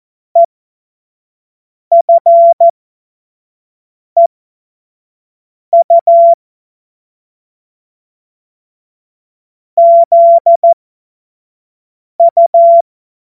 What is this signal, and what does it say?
Morse code, heard as a keyed tone, about 14 wpm, spaced as for 5 wpm, 680 Hz. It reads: EFEU ZU
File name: Morsekurs / Witze / Witz 27. Januar 2026 25 F70